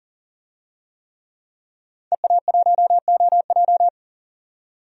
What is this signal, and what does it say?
Morse code, heard as a keyed tone, about 40 wpm, 700 Hz. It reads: EA1OJ
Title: DXCC-100 Call Signs - 5 Characters - 40wpm